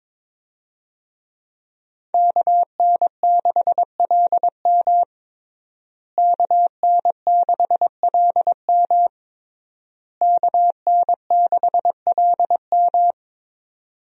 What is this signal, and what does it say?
Morse code, heard as a keyed tone, about 22 wpm, 700 Hz. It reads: KN6LM KN6LM KN6LM
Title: USA Call Signs - 5 Characters - Repeated 3x - 22wpm